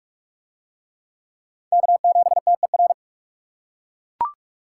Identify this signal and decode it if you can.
Morse code, heard as a keyed tone, about 39 wpm, 700 Hz. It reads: K7TER